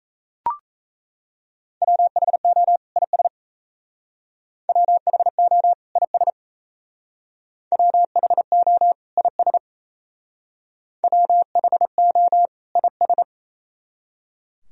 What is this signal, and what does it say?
Morse code, heard as a keyed tone, about 35 wpm, 700 Hz. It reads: WHO IS WHO IS WHO IS WHO IS
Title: Speed-Racing - Sets of 2 Words 28wpm